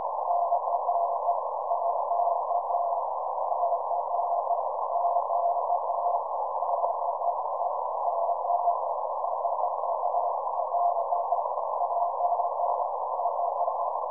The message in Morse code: none